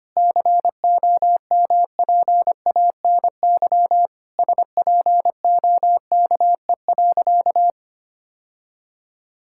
Morse code, heard as a keyed tone, about 25 wpm, 700 Hz. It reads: COMPANY SPOKE.
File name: practice / 25wpm / Sentences with 3-Word Phrases - 25wpm